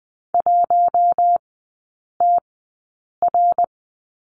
Morse code, heard as a keyed tone, about 20 wpm, spaced as for 8 wpm, 700 Hz. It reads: 1TR
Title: 067 Lesson 12e - Letter d - Three Characters 20wpm